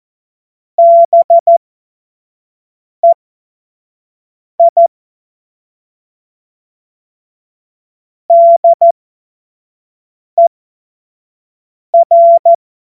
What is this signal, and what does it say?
Morse code, heard as a keyed tone, about 14 wpm, spaced as for 5 wpm, 680 Hz. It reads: BEI DER
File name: Morsekurs / Witze / Witz 08. August 2023 25 F70